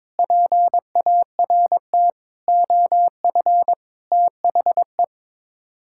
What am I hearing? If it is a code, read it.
Morse code, heard as a keyed tone, about 22 wpm, 700 Hz. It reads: PART OF THE
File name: practice / 22wpm / Sets of 6 Words - 22wpm